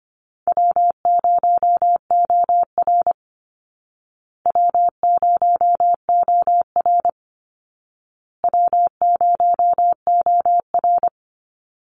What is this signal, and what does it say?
Morse code, heard as a keyed tone, about 25 wpm, 700 Hz. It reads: W0OR W0OR W0OR
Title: Easy Call Signs Repeated 3x - 25wpm